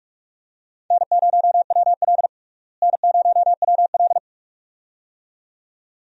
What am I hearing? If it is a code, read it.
Morse code, heard as a keyed tone, about 46 wpm, 700 Hz. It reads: N0WL N0WL
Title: USA Call Signs - 4 Characters - Repeated 2x - 45wpm